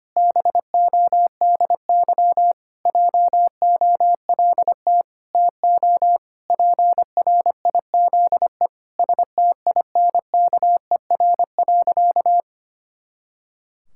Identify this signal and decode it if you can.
Morse code, heard as a keyed tone, about 25 wpm, 700 Hz. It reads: BODY JOLT TO PRIZE STINKER.
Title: Pangram Challenge v2 - 25wpm